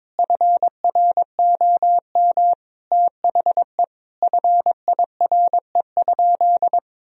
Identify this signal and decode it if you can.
Morse code, heard as a keyed tone, about 22 wpm, 700 Hz. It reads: FROM THE FIRE?
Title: Sentences from Top 1000 Words - Review Part 3 - 22wpm